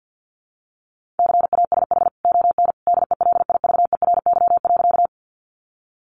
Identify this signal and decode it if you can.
Morse code, heard as a keyed tone, about 50 wpm, 700 Hz. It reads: CASH ON DELIVERY.
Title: Sentences from Top 1500 Words 50wpm